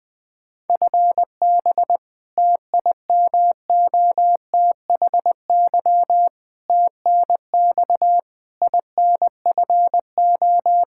Morse code, heard as a keyed tone, about 21 wpm, 700 Hz. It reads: FB TIMOTHY TNX INFO
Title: Practice QSO - Exchange 3 - More Info - 20wpm